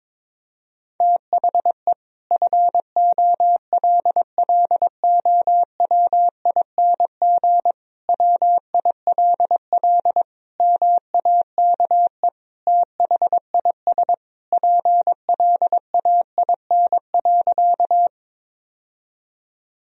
Morse code, heard as a keyed tone, about 22 wpm, 700 Hz. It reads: THE FOLLOWING WILL MAKE THIS PLAIN.
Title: Sentences from Top 1000 Words - Review Part 4 - 22wpm